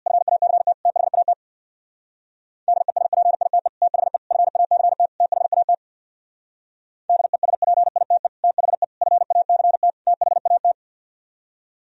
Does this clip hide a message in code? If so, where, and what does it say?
Yes: Morse code from 0.1 to 10.7 s, heard as a keyed tone, about 64 wpm, 700 Hz. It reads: FACT THAT DESPITE THE FACT THAT DESPITE THE FACT THAT